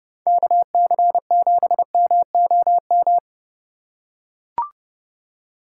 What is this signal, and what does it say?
Morse code, heard as a keyed tone, about 30 wpm, 700 Hz. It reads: KC7MOM